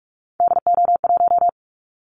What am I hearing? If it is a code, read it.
Morse code, heard as a keyed tone, about 45 wpm, 700 Hz. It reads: DO1
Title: Callsign Prefixes for Top 50 DXCC - 45wpm